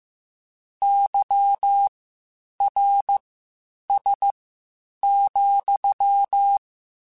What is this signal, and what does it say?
Morse code, heard as a keyed tone, about 15 wpm, spaced as for 8 wpm, 800 Hz. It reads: YRS,